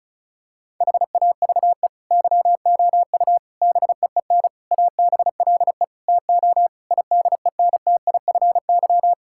Morse code, heard as a keyed tone, about 36 wpm, 700 Hz. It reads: HAVE YOU BEEN ABLE TO IDENTIFY
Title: Sentences from Top 1000 Words - Review Part 6 - 35wpm